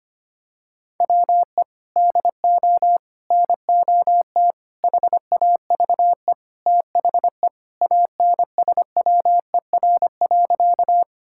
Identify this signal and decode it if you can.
Morse code, heard as a keyed tone, about 25 wpm, 700 Hz. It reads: WE DO NOT HAVE THE ANSWER.
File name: practice / 25wpm / Sentences with 4-Word Phrases - 25wpm